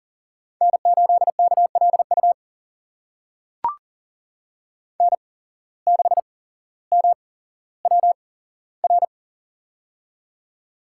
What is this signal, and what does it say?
Morse code, heard as a keyed tone, about 40 wpm, spaced as for 17 wpm, 700 Hz. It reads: N8KLU N6MWR
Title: USA Call Signs - 5 Characters - 8x Character Spacing - 40wpm